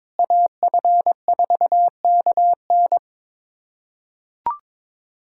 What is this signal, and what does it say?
Morse code, heard as a keyed tone, about 22 wpm, 700 Hz. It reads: AF4KN
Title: USA Callsign DE Callsign 22wpm